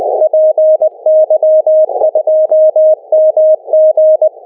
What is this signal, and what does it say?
Morse code, heard as a keyed tone, about 20 wpm, 620 Hz. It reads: PY2MG
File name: py2mg / Arquivos PDF / Transceptor fenix RX PArdinho